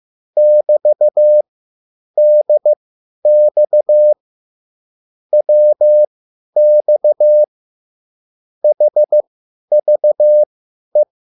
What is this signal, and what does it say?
Morse code, heard as a keyed tone, about 15 wpm, spaced as for 10 wpm, 600 Hz. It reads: <BT>DX WX HVE
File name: QSL / CW lesson